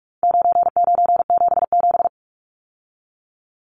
Morse code, heard as a keyed tone, about 44 wpm, 700 Hz. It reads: 9977
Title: Numbers - 4-Digit 45wpm